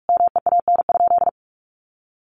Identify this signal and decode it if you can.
Morse code, heard as a keyed tone, about 44 wpm, 700 Hz. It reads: MEAN?
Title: Sentences from Top 1300 Words 45wpm